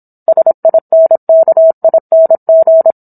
Morse code, heard as a keyed tone, about 27 wpm, 650 Hz. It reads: SINKING
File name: Book One / 17-Chapter 17 - 26 WPM